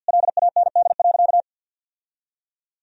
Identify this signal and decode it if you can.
Morse code, heard as a keyed tone, about 51 wpm, 700 Hz. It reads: LAND.
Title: Sentences from Top 1000 Words - Review Part 1 - 50wpm